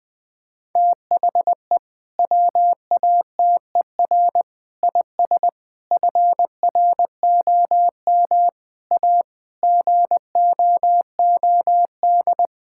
Morse code, heard as a keyed tone, about 20 wpm, 700 Hz. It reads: THE WATER IS FROM A GOOD
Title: Sentences from Top 1000 Words - Review Part 5 - 20wpm